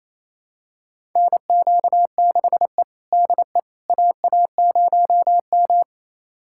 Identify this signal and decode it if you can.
Morse code, heard as a keyed tone, about 28 wpm, 700 Hz. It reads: NQ6E DE AA0M
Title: USA Callsign DE Callsign - 4 Characters - 28wpm